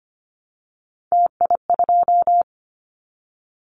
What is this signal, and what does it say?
Morse code, heard as a keyed tone, about 25 wpm, 700 Hz. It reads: TI2
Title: Callsign Prefixes for Top 100 DXCC - 25wpm